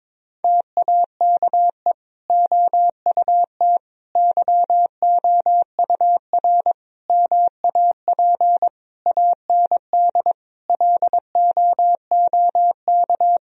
Morse code, heard as a keyed tone, about 22 wpm, 700 Hz. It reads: TAKE OUT YOUR MAP AND LOOK
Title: Sentences from Top 1000 Words - Review Part 2 - 22wpm